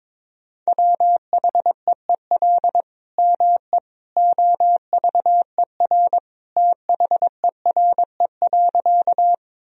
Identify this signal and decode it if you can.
Morse code, heard as a keyed tone, about 21 wpm, 700 Hz. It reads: WHEEL ME OVER THERE.
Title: Sentences from Top 1000 Words - Review Part 3 - 22wpm